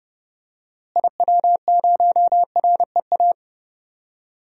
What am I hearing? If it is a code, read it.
Morse code, heard as a keyed tone, about 30 wpm, 700 Hz. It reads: IW0REA
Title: DXCC-100 Call Signs - 6 Characters - Repeated 2x - 30wpm